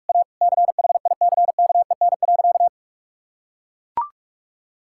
Morse code, heard as a keyed tone, about 41 wpm, 700 Hz. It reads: A CHICKEN.
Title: Sentences from Top 1000 Words - Review Part 5 - 45wpm